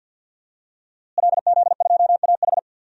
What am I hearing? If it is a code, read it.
Morse code, heard as a keyed tone, about 51 wpm, 700 Hz. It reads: LZ2AH
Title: DXCC-100 Call Signs - 5 Characters - Repeated 2x - 50wpm